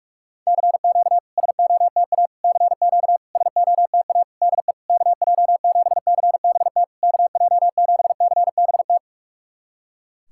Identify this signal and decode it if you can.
Morse code, heard as a keyed tone, about 46 wpm, 700 Hz. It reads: CQ SOTA CQ SOTA DE KJ7CBT KJ7CBT
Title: SOTA QSO Practice 45wpm